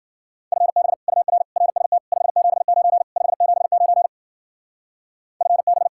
Morse code, heard as a keyed tone, about 52 wpm, 700 Hz. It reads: FB UR RST 579 579 FB